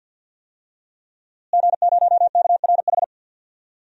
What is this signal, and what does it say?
Morse code, heard as a keyed tone, about 50 wpm, 700 Hz. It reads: G0KRH
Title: DXCC-100 Call Signs - 5 Characters - Repeated 2x - 50wpm